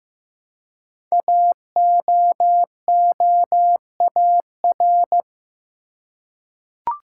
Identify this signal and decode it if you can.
Morse code, heard as a keyed tone, about 15 wpm, 700 Hz. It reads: AOOAR
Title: Code Groups - 5 Letter:Numbers - 15wpm